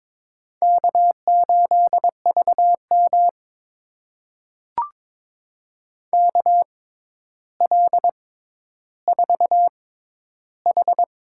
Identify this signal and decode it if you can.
Morse code, heard as a keyed tone, about 22 wpm, spaced as for 11 wpm, 700 Hz. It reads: K8VM KL4H
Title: USA Call Signs - 4 Characters - 6x Character Spacing - 22wpm